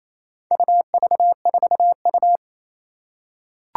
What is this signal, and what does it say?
Morse code, heard as a keyed tone, about 29 wpm, 700 Hz. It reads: UV4U
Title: Four Challenging Letter-Number Combinations 28wpm